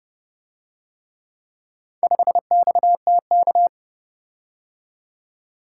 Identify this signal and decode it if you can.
Morse code, heard as a keyed tone, about 29 wpm, 700 Hz. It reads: 5XTK